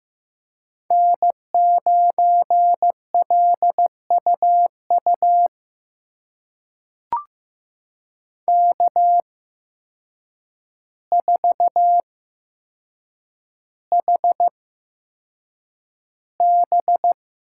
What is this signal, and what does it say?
Morse code, heard as a keyed tone, about 15 wpm, spaced as for 7 wpm, 700 Hz. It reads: N9LUU K4HB
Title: USA Call Signs - 5 Characters - 8x Character Spacing - 15wpm